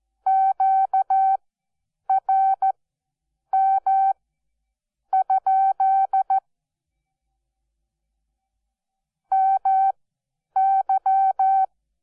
Morse code, heard as a keyed tone, about 14 wpm, spaced as for 8 wpm, 770 Hz. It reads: QRM? MY
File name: Onlinelehrgang / morsen / DJ4UF Morsen12